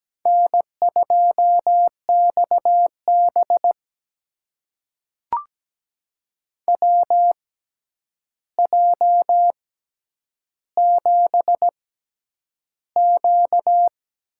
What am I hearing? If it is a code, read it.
Morse code, heard as a keyed tone, about 17 wpm, spaced as for 9 wpm, 700 Hz. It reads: N2XB WJ7Q
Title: USA Call Signs - 4 Characters - 6x Character Spacing - 17wpm